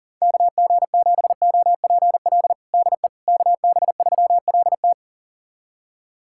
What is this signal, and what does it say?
Morse code, heard as a keyed tone, about 40 wpm, 700 Hz. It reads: KG7OPL DE KB3LT
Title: USA Callsign DE Callsign 40wpm